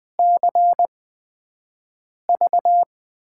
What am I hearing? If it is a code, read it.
Morse code, heard as a keyed tone, about 20 wpm, spaced as for 5 wpm, 700 Hz. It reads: CV
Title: USA Call Signs - 6 Characters - 8x Character Spacing - 20wpm